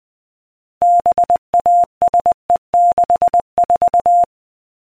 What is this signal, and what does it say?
Morse code, heard as a keyed tone, about 20 wpm, 700 Hz. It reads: BASE64